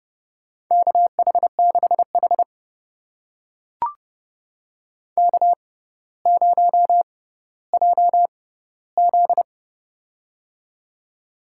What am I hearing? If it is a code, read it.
Morse code, heard as a keyed tone, about 30 wpm, spaced as for 15 wpm, 700 Hz. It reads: KH6H K0JZ
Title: USA Call Signs - 4 Characters - 6x Character Spacing - 30wpm